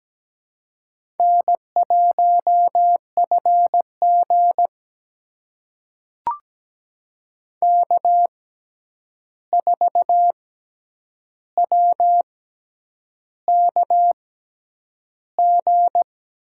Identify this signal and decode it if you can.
Morse code, heard as a keyed tone, about 17 wpm, spaced as for 8 wpm, 700 Hz. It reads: N1FG K4WKG
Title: USA Call Signs - 6x Character Spacing - 17wpm